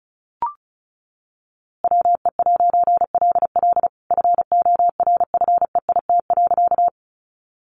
Morse code, heard as a keyed tone, about 34 wpm, 700 Hz. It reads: WE'LL FORFEIT.
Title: A Sound Of Thunder 35wpm